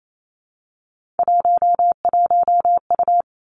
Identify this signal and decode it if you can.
Morse code, heard as a keyed tone, about 28 wpm, 690 Hz. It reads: 11U
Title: Three Challenging Letter-Number Combinations 28wpm